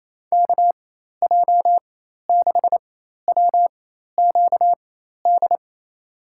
Morse code, heard as a keyed tone, about 28 wpm, spaced as for 13 wpm, 700 Hz. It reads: KJ6WQD
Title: USA Call Signs - 6 Characters - 4x Character Spacing - 28wpm